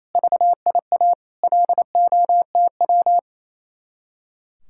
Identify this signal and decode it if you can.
Morse code, heard as a keyed tone, about 36 wpm, 700 Hz. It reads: VIA LOTW